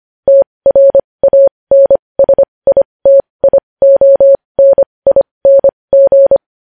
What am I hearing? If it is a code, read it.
Morse code, heard as a keyed tone, about 25 wpm, spaced as for 20 wpm, 550 Hz. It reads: TRANSITIONING
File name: podcast / News headlines-1x-25x20